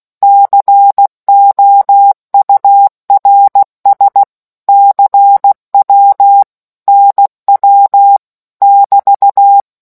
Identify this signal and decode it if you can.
Morse code, heard as a keyed tone, about 16 wpm, 800 Hz. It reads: COURS CW NW <BT>